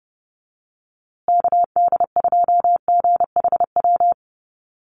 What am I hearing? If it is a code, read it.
Morse code, heard as a keyed tone, about 30 wpm, 700 Hz. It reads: KD2GHW
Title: USA Call Signs Repeated 3x 30wpm